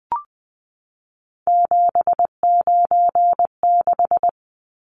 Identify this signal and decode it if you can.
Morse code, heard as a keyed tone, about 20 wpm, 700 Hz. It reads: 796
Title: Numbers - 3-Digit 20wpm